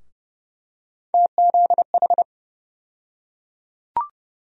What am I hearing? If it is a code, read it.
Morse code, heard as a keyed tone, about 30 wpm, 700 Hz. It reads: TZH
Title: USA Call Sign Postfixes - 3 Letters 30wpm